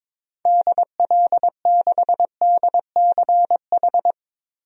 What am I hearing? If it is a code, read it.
Morse code, heard as a keyed tone, about 22 wpm, 700 Hz. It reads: DL6DCH
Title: DXCC-100 Call Signs - 6 Characters - Repeated 3x - 22wpm